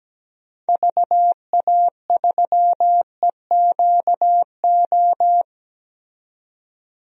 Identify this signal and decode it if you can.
Morse code, heard as a keyed tone, about 17 wpm, 700 Hz. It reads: VA3EQO
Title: DXCC-100 Call Signs - 6 Characters - 17wpm